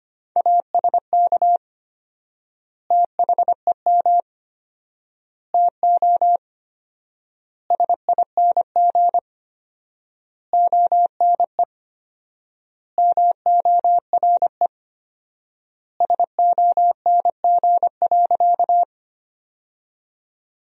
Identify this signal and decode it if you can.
Morse code, heard as a keyed tone, about 25 wpm, 700 Hz. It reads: ASK THEM TO SING ONE MORE SONG.
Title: 4x Word-Spacing - Sentences from Top 400 Words - 25wpm